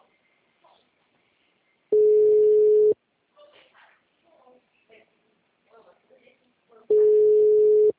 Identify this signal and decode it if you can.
Ringback tone (Europe, 425 Hz), with speech.